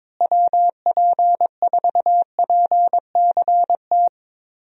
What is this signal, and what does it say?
Morse code, heard as a keyed tone, about 22 wpm, 700 Hz. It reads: WP4PCT
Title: USA Call Signs - 6 Characters - Repeated 3x - 22wpm